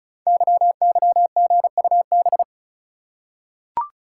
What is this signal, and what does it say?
Morse code, heard as a keyed tone, about 36 wpm, 690 Hz. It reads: YYGUB
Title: Code Groups - 5 Letters - 35wpm